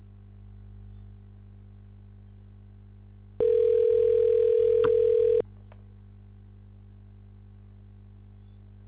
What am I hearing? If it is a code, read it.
Ringback tone (North America), 440 + 480 Hz.